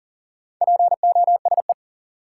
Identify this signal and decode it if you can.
Morse code, heard as a keyed tone, about 40 wpm, 700 Hz. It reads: POSE
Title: Sentences from Top 1000 Words - Review Part 2 - New Word Drill - 40wpm